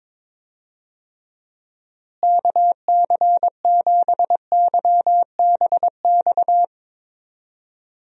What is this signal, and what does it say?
Morse code, heard as a keyed tone, about 22 wpm, 700 Hz. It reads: KC7YBX
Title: USA Call Signs - 6 Characters - Repeated 3x - 22wpm